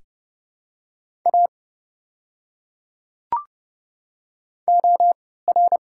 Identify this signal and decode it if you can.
Morse code, heard as a keyed tone, about 31 wpm, spaced as for 17 wpm, 700 Hz. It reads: A OR
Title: 3x Character Spacing - Top 200 Words - 30wpm